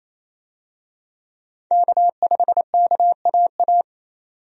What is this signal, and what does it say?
Morse code, heard as a keyed tone, about 27 wpm, 700 Hz. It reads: K5KAA
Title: USA Call Signs - 5 Characters - Repeated 2x - 28wpm